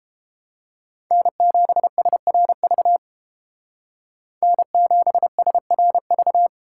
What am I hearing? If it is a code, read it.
Morse code, heard as a keyed tone, about 31 wpm, 700 Hz. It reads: N7SRV N7SRV